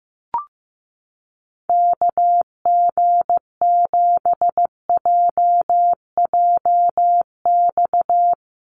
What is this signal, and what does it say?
Morse code, heard as a keyed tone, about 15 wpm, 700 Hz. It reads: KG7JJX
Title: USA Call Signs Repeated 2x 15wpm